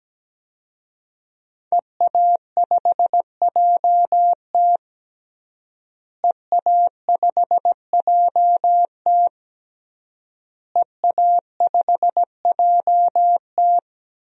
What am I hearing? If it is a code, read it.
Morse code, heard as a keyed tone, about 17 wpm, 700 Hz. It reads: EA5JT EA5JT EA5JT
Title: DXCC-100 Call Signs - 5 Characters - Repeated 3x - 17wpm